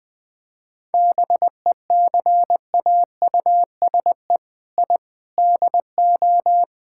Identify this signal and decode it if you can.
Morse code, heard as a keyed tone, about 20 wpm, 700 Hz. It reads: BECAUSE I DO
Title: Sentences with 4-Word Phrases - 20wpm